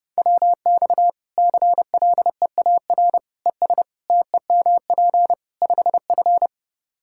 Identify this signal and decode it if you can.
Morse code, heard as a keyed tone, about 30 wpm, 700 Hz. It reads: WX CLEAR ES TEMP 5F
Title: Practice QSO - Exchange 2 - Info - 30wpm